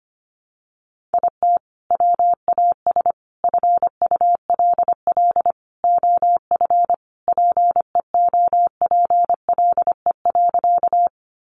Morse code, heard as a keyed tone, about 25 wpm, 700 Hz. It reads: IT WAS FULL OF PEOPLE.
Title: Sentences from Top 500 Words 25wpm